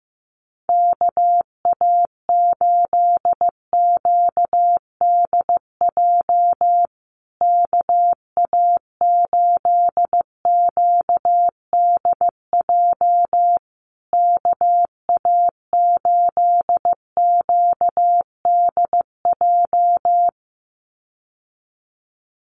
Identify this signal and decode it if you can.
Morse code, heard as a keyed tone, about 15 wpm, 700 Hz. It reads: KA8QDJ KA8QDJ KA8QDJ
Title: USA Call Signs Repeated 3x 15wpm